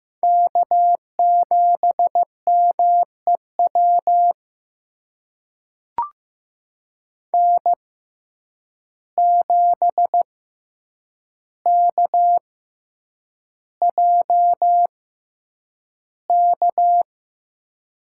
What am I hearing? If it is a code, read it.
Morse code, heard as a keyed tone, about 15 wpm, spaced as for 8 wpm, 700 Hz. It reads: K7MEW N7KJK
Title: USA Call Signs - 6x Character Spacing - 15wpm